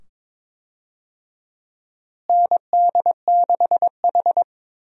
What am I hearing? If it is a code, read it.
Morse code, heard as a keyed tone, about 22 wpm, 700 Hz. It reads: ND6H
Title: USA Call Signs - 4 Characters - Repeated 2x - 22wpm